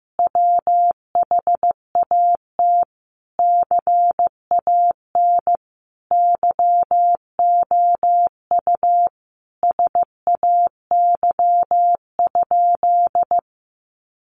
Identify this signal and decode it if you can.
Morse code, heard as a keyed tone, about 15 wpm, 700 Hz. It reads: WHAT CAN YOU SAY?
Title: Sentences from Top 800 Words 15wpm